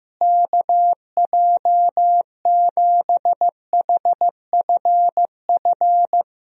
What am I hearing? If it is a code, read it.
Morse code, heard as a keyed tone, about 15 wpm, 700 Hz. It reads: KJ7HFF
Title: USA Call Signs - 6 Characters - Repeated 3x - 15wpm